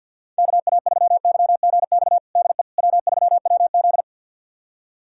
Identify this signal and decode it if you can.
Morse code, heard as a keyed tone, about 52 wpm, 700 Hz. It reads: KA3YGX DE W3WZ
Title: USA Callsign DE Callsign 50wpm